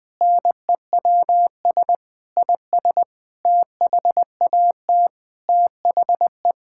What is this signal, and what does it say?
Morse code, heard as a keyed tone, about 21 wpm, 700 Hz. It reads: NEWS IS THAT THE
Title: Sentences with 6-Word Phrases - Spoken Beginning - 20wpm